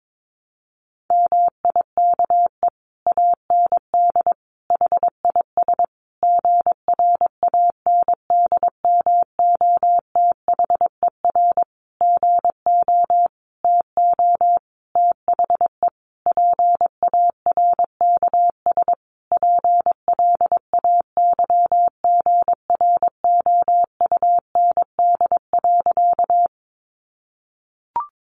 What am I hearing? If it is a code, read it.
Morse code, heard as a keyed tone, about 22 wpm, 700 Hz. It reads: MIKE AND HIS GRANDMOTHER GO TO THE PARKS PLAYGROUND.
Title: Additional Easy Phrases 22wpm